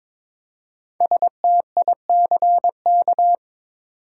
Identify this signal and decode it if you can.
Morse code, heard as a keyed tone, about 22 wpm, 700 Hz. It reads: STICK
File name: 5 Letter Words - First 3 Letters Given - 22wpm